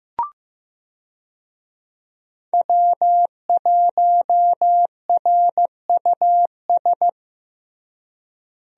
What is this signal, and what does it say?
Morse code, heard as a keyed tone, about 15 wpm, 700 Hz. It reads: W1RUS
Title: USA Call Signs - 5 Characters - Repeated 2x - 15wpm